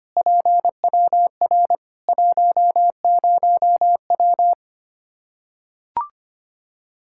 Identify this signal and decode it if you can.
Morse code, heard as a keyed tone, about 25 wpm, 700 Hz. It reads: PWR 10W
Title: Practice QSO - Exchange 2 - Info - 25wpm